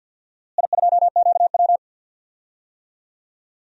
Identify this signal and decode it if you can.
Morse code, heard as a keyed tone, about 52 wpm, 700 Hz. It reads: I2QW
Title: DXCC-100 Call Signs - 4 Characters - Repeated 2x - 50wpm